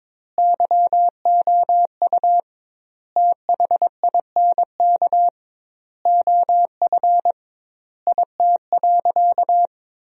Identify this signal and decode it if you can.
Morse code, heard as a keyed tone, about 21 wpm, 700 Hz. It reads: YOU THINK OF IT.
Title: 2x Word-Spacing - Sentences from Top 200 Words - 22wpm